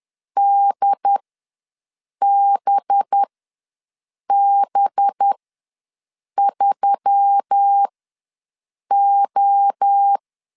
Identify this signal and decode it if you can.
Morse code, heard as a keyed tone, about 10 wpm, spaced as for 6 wpm, 800 Hz. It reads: DBB3O
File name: LEZIONI / 5 LBVC / 5 LB ws 10wpm 3616